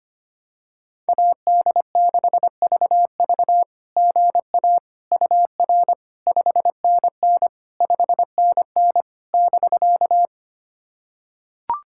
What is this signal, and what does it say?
Morse code, heard as a keyed tone, about 25 wpm, 700 Hz. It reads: AD6VV GA UR 5NN 5NN <BK>
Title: SOTA QSO Practice 25wpm